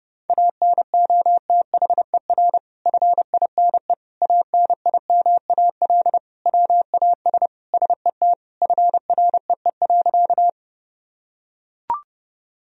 Morse code, heard as a keyed tone, about 30 wpm, 700 Hz. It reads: ANOTHER FINE ANIMAL WAS SET FREE.
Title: Sentences from Top 1000 Words - Review Part 1 - 30wpm